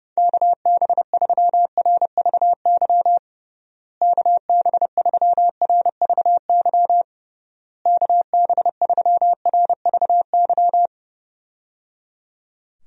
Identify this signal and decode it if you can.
Morse code, heard as a keyed tone, about 30 wpm, 700 Hz. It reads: KB3RVY KB3RVY KB3RVY
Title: USA Call Signs - 6 Characters - Repeated 3x - 30wpm